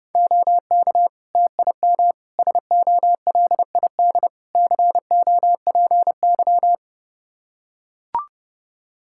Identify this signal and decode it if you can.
Morse code, heard as a keyed tone, about 30 wpm, 700 Hz. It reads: OK TIM SOLID COPY
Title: Practice QSO - Exchange 3 - More Info - 30wpm